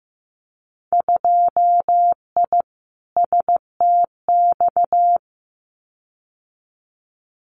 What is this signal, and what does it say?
Morse code, heard as a keyed tone, about 15 wpm, 700 Hz. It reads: 2I STX
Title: Winter Field Day Exchanges 15wpm